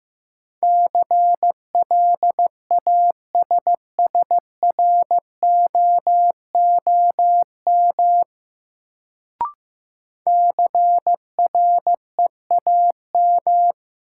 Morse code, heard as a keyed tone, about 15 wpm, 700 Hz. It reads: CLASSROOM CREAM
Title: Sentences from Top 1500 Words - New Word Drill Alt - 15wpm